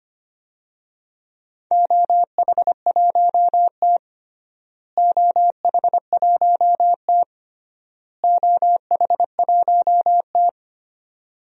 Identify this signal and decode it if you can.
Morse code, heard as a keyed tone, about 25 wpm, 700 Hz. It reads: OH1T OH1T OH1T
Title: DXCC-100 Call Signs - 4 Characters - Repeated 3x - 25wpm